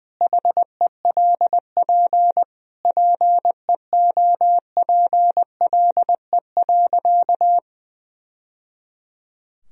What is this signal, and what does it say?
Morse code, heard as a keyed tone, about 21 wpm, 700 Hz. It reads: HELP PEOPLE.
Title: Sentences from Top 1000 Words - Review Part 4 - 20wpm